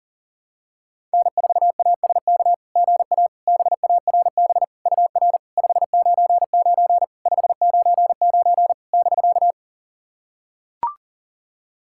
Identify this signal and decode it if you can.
Morse code, heard as a keyed tone, about 40 wpm, 690 Hz. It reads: N4ASK GA BARB UR 599 599 <BK>
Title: SOTA QSO Practice 40wpm